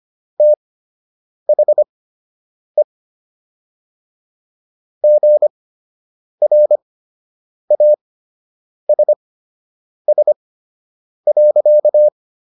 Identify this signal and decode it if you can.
Morse code, heard as a keyed tone, about 25 wpm, spaced as for 8 wpm, 600 Hz. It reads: THE GRASS.